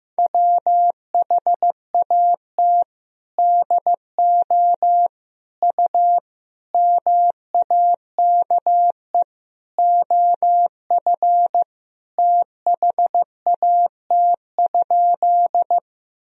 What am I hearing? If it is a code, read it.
Morse code, heard as a keyed tone, about 16 wpm, 700 Hz. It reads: WHAT DO U MAKE OF THAT?
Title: Sentences from Top 100 Words and Abbreviations 15wpm